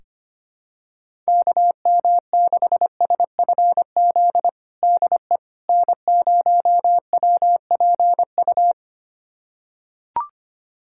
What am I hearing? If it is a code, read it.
Morse code, heard as a keyed tone, about 25 wpm, 700 Hz. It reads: KM6SFZ DE N0WPU
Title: USA Callsign DE Callsign 25wpm